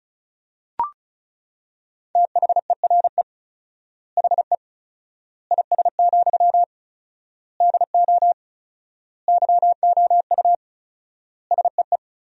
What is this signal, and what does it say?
Morse code, heard as a keyed tone, about 34 wpm, 700 Hz. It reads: THERE HE IS, DO YOU SEE
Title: 4x Word-Spacing - Sentences from Top 100 Words - 35wpm